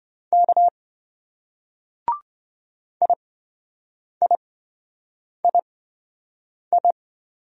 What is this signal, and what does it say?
Morse code, heard as a keyed tone, about 26 wpm, spaced as for 7 wpm, 700 Hz. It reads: K IIII